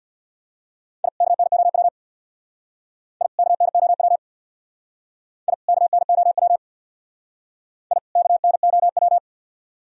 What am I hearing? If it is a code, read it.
Morse code, heard as a keyed tone, about 58 wpm, 690 Hz. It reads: I KNOW I KNOW I KNOW I KNOW
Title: Speed-Racing - Sets of 2 Words 50wpm